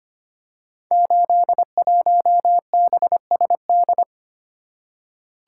Morse code, heard as a keyed tone, about 25 wpm, 700 Hz. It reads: 81BSD
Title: Code Groups - 5 Letter:Numbers - 25wpm